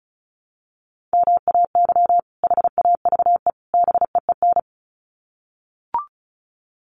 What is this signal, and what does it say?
Morse code, heard as a keyed tone, about 34 wpm, 700 Hz. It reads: MAY HAVE BEEN
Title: Sets of 3 Words 35wpm